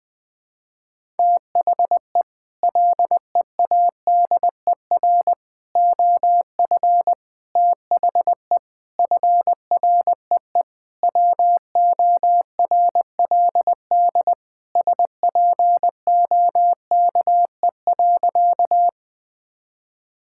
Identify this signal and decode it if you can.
Morse code, heard as a keyed tone, about 21 wpm, 700 Hz. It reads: THE LEADER OF THE FREE WORLD SPOKE.
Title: Sentences with 6-Word Phrases - 20wpm